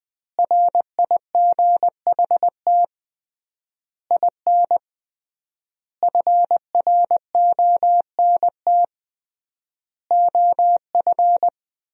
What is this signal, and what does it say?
Morse code, heard as a keyed tone, about 20 wpm, 700 Hz. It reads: RIGHT IN FRONT OF
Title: 3x Word-Spacing - Sentences from Top 500 Words - 20wpm